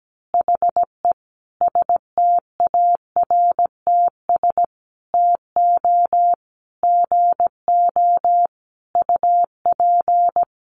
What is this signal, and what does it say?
Morse code, heard as a keyed tone, about 17 wpm, 700 Hz. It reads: HE STARTS TO GO UP
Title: Additional Easy Phrases 17wpm